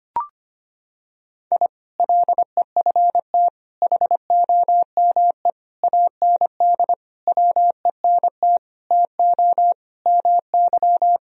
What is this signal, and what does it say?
Morse code, heard as a keyed tone, about 25 wpm, 700 Hz. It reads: I LEFT HOME AND WENT TO MY
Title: Sentences from Top 1000 Words - Review Part 3 - 25wpm